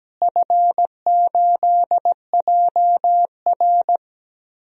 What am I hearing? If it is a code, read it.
Morse code, heard as a keyed tone, about 17 wpm, 700 Hz. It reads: F8JR
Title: DXCC-100 Call Signs - 4 Characters - Repeated 3x - 17wpm